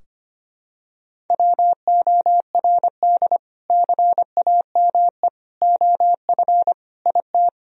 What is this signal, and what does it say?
Morse code, heard as a keyed tone, about 25 wpm, 700 Hz. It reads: WORD CAME OF IT
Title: Sentences from Top 1000 Words - Review Part 4 - 25wpm